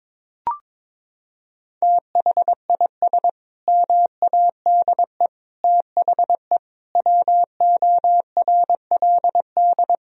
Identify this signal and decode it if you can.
Morse code, heard as a keyed tone, about 22 wpm, 700 Hz. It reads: THIS MADE THE WORLD
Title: Sentences from Top 1000 Words - Review Part 2 - 22wpm